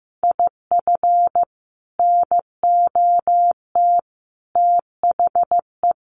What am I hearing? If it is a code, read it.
Morse code, heard as a keyed tone, about 15 wpm, 700 Hz. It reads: IF NOT THE
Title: Sentences from Top 1400 Words 15wpm